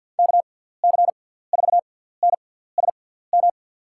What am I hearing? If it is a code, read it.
Morse code, heard as a keyed tone, about 51 wpm, spaced as for 17 wpm, 700 Hz. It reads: KC4NSM
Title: USA Call Signs - 6 Characters - 6x Character Spacing - 50wpm